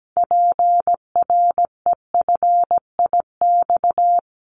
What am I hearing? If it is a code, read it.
Morse code, heard as a keyed tone, about 17 wpm, 700 Hz. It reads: PREFIX
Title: Callsign Prefixes for Top 25 DXCC - Repeated 3x - 17wpm